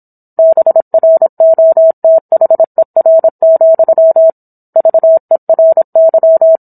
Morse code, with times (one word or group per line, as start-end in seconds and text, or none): 0.4-4.3 BROTHER,
4.8-6.6 VERY